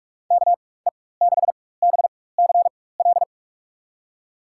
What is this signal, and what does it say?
Morse code, heard as a keyed tone, about 44 wpm, spaced as for 21 wpm, 700 Hz. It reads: KE6BCL
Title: USA Call Signs - 6 Characters - 4x Character Spacing - 45wpm